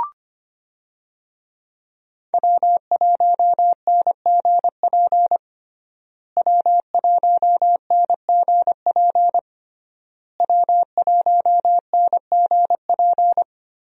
Morse code, heard as a keyed tone, about 25 wpm, 700 Hz. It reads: W1NGP W1NGP W1NGP
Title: USA Call Signs - 5 Characters - Repeated 3x - 25wpm